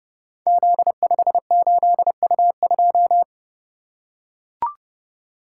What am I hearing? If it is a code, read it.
Morse code, heard as a keyed tone, about 31 wpm, 700 Hz. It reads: Z58U2
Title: Code Groups - 5 Letter:Numbers - 30wpm